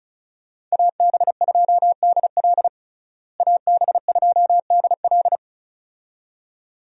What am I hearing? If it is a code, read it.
Morse code, heard as a keyed tone, about 36 wpm, 700 Hz. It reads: AB2DL AB2DL
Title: USA Call Signs - 5 Characters - Repeated 2x - 35wpm